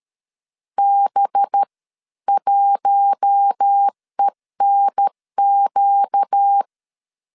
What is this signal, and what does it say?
Morse code, heard as a keyed tone, about 13 wpm, spaced as for 11 wpm, 800 Hz. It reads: B1ENQ
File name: LEZIONI / 7 GF9Z / 7 GF 12wpm 2122